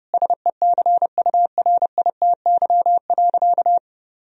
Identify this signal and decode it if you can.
Morse code, heard as a keyed tone, about 30 wpm, 700 Hz. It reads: SECURITY.
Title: Sentences from Top 1000 Words - Review Part 4 - 30wpm